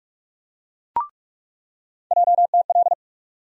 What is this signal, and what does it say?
Morse code, heard as a keyed tone, about 44 wpm, 700 Hz. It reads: JTL
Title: USA Call Sign Postfixes - 3 Letters 45wpm